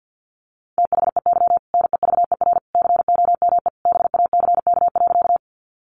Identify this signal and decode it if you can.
Morse code, heard as a keyed tone, about 50 wpm, 700 Hz. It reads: THEY NEVER COME BACK.
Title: Sentences from Top 300 Words 50wpm